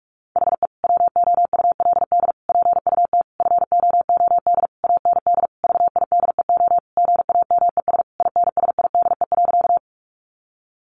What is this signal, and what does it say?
Morse code, heard as a keyed tone, about 46 wpm, 700 Hz. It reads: HE WOULD PUT FOOD AND VIDEO GAMES INSIDE.
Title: Easy Phrases 45wpm